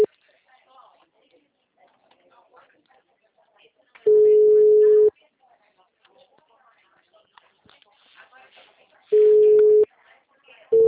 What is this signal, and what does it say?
Ringback tone (Europe, 425 Hz), with speech.